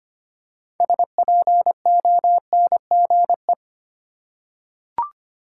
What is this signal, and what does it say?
Morse code, heard as a keyed tone, about 25 wpm, 690 Hz. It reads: SPONGE
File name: practice / 25wpm / Easy Words with Hints - 25wpm